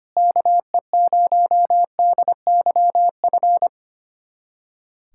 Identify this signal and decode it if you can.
Morse code, heard as a keyed tone, about 25 wpm, 700 Hz. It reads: KE0DYF
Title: USA Callsign DE Callsign 25wpm